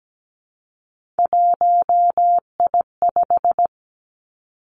Morse code, heard as a keyed tone, about 17 wpm, 700 Hz. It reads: 1I5
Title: Three Challenging Letter-Number Combinations 17wpm